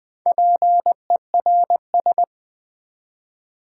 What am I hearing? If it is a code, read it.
Morse code, heard as a keyed tone, about 20 wpm, 700 Hz. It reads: PERS
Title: Build A Word - 4 Letters At A Time - 10-Letter Words - 20wpm